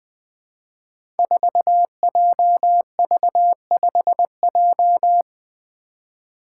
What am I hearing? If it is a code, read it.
Morse code, heard as a keyed tone, about 20 wpm, 700 Hz. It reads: 4JV5J
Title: Five Challenging Letter-Number Combinations 20wpm